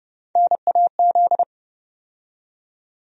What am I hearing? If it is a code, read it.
Morse code, heard as a keyed tone, about 30 wpm, 700 Hz. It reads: NAZ
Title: USA Call Sign Postfixes - 3 Letters 30wpm